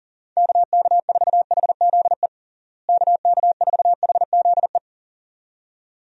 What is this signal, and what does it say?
Morse code, heard as a keyed tone, about 40 wpm, 700 Hz. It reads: KK4HZE KK4HZE
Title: USA Call Signs - 6 Characters - Repeated 2x - 40wpm